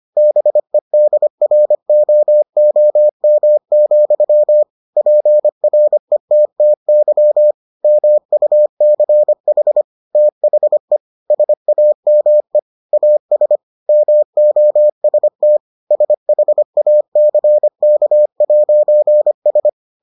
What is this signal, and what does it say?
Morse code, heard as a keyed tone, about 25 wpm, 600 Hz. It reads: BEDROOM, PRETTY MUCH THE SAME AS MOST SHACK'S